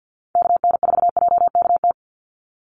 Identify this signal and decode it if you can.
Morse code, heard as a keyed tone, about 48 wpm, 700 Hz. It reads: KN4JKT
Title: USA Call Signs 50wpm